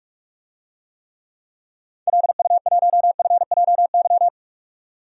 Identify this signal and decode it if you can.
Morse code, heard as a keyed tone, about 43 wpm, 700 Hz. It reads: LU1FJY
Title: DXCC-100 Call Signs - 6 Characters - 45wpm